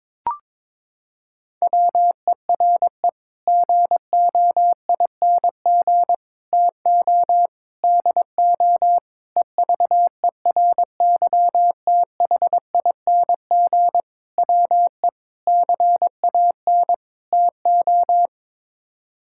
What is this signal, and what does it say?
Morse code, heard as a keyed tone, about 22 wpm, 700 Hz. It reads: WERE GOING TO DO EVERYTHING WE CAN TO
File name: practice / 22wpm / Sets of 8 Words - 22wpm